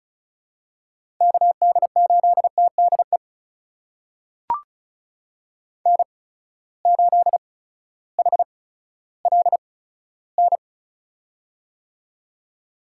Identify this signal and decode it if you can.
Morse code, heard as a keyed tone, about 36 wpm, spaced as for 16 wpm, 700 Hz. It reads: KD8TDE N8HLN
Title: USA Call Signs - 8x Character Spacing - 35wpm